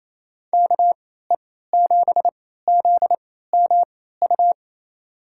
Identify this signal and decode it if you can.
Morse code, heard as a keyed tone, about 28 wpm, spaced as for 16 wpm, 700 Hz. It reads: KE7ZMU
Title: USA Call Signs - 3x Character Spacing - 28wpm